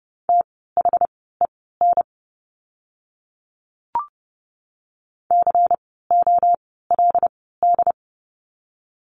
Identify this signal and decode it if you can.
Morse code, heard as a keyed tone, about 30 wpm, spaced as for 17 wpm, 700 Hz. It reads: THEN COLD
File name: Lesson 16f - Letter c - Words 30wpm